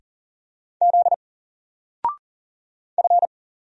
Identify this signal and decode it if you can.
Morse code, heard as a keyed tone, about 40 wpm, 700 Hz. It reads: Z F
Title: Single Letter-Number 40wpm